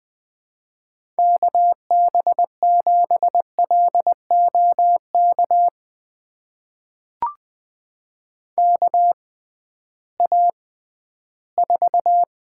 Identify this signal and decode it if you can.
Morse code, heard as a keyed tone, about 20 wpm, spaced as for 13 wpm, 700 Hz. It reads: KB7LOK KA4
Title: USA Call Signs - 6 Characters - 6x Character Spacing - 20wpm